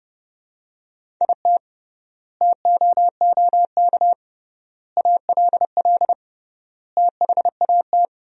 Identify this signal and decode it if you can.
Morse code, heard as a keyed tone, about 30 wpm, 700 Hz. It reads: IT TOOK ALL THAT
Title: 3x Word-Spacing - Sentences from Top 300 Words - 30wpm